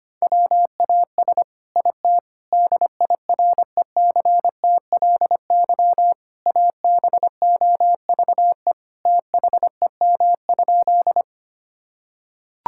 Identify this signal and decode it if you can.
Morse code, heard as a keyed tone, about 25 wpm, 700 Hz. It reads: WAS IT DIRECTLY ABOVE THEM?
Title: Sentences from Top 1000 Words - Review Part 5 - 25wpm